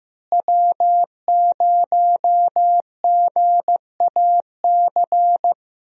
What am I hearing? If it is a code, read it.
Morse code, heard as a keyed tone, about 15 wpm, 700 Hz. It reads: W0GAC
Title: USA Call Signs - 5 Characters - Repeated 2x - 15wpm